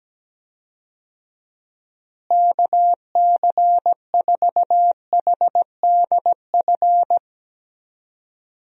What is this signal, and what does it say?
Morse code, heard as a keyed tone, about 17 wpm, 700 Hz. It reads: KC4HDF
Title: USA Call Signs - 6 Characters - Repeated 3x - 17wpm